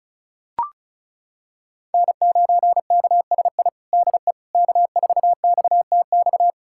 Morse code, heard as a keyed tone, about 36 wpm, 700 Hz. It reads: N9KSI DE K4XTX
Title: USA Callsign DE Callsign - 5 Characters - 35wpm